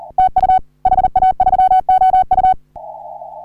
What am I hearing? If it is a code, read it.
Morse code, heard as a keyed tone, about 38 wpm, 750 Hz. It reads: TU HA3OU
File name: CW / HA3OU RHOBCW REC